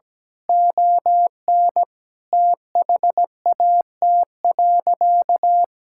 Morse code, heard as a keyed tone, about 17 wpm, 700 Hz. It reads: ON THAT.
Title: Sentences from Top 1000 Words - Review Part 5 - 17wpm